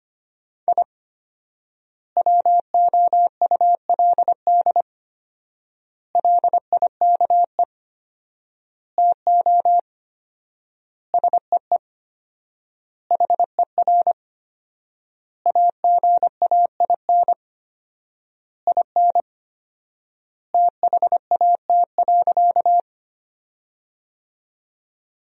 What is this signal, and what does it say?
Morse code, heard as a keyed tone, about 25 wpm, 700 Hz. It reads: I WOULD LIKE TO SEE HER AGAIN IN THAT.
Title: 4x Word-Spacing - Sentences from Top 200 Words - 25wpm